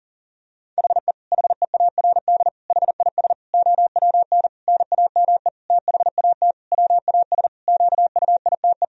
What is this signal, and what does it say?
Morse code, heard as a keyed tone, about 40 wpm, 700 Hz. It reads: HE HEARD HIS OWN NAME THAT WAS QUITE
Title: Sentences from Top 1000 Words - Review Part 2 - 40wpm